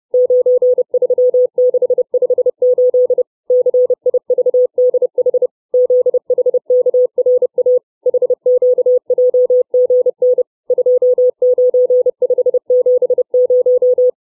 Morse code, heard as a keyed tone, about 30 wpm, 500 Hz. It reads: 93658 CIVDH ZHKRA HQJGN 29570
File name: CW / letter number groups 4